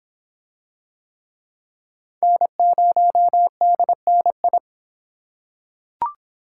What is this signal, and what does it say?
Morse code, heard as a keyed tone, about 26 wpm, 700 Hz. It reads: N0DNI